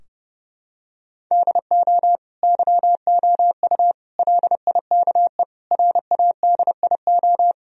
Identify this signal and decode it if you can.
Morse code, heard as a keyed tone, about 32 wpm, 700 Hz. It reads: DO YOU LIKE RADIO
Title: Sentences from Top 1000 Words - Review Part 4 - 30wpm